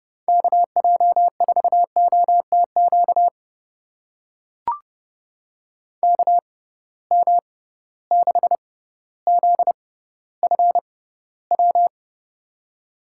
Morse code, heard as a keyed tone, about 30 wpm, spaced as for 15 wpm, 700 Hz. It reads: KJ4OTQ KM6ZFW
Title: USA Call Signs - 6 Characters - 6x Character Spacing - 30wpm